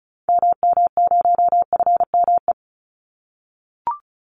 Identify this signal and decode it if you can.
Morse code, heard as a keyed tone, about 34 wpm, 700 Hz. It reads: MM0FME
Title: Call Signs - 35wpm